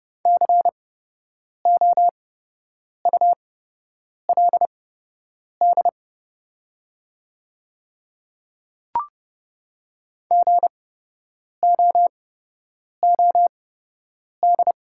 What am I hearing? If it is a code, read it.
Morse code, heard as a keyed tone, about 30 wpm, spaced as for 8 wpm, 700 Hz. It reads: COULD GOOD
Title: 8x Character Spacing - Top 300 Words - 30wpm